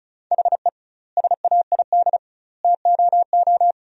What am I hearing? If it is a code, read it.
Morse code, heard as a keyed tone, about 36 wpm, 700 Hz. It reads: HE SAID TOO
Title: 2x Word-Spacing - Sentences from Top 200 Words - 35wpm